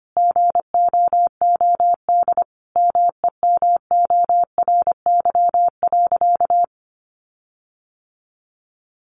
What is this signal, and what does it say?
Morse code, heard as a keyed tone, about 25 wpm, 700 Hz. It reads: GOOD MEMORY.
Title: Sentences from Top 1300 Words 25wpm